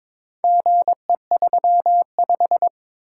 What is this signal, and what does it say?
Morse code, heard as a keyed tone, about 22 wpm, 700 Hz. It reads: GE35
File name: practice / 22wpm / Four Letter-Number Combinations - 22wpm